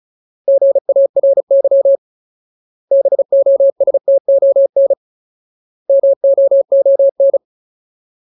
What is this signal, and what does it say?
Morse code, heard as a keyed tone, about 34 wpm, 550 Hz. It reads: GARY BOSTON MOON
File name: Ultimate Word List / 35WPM Ultimate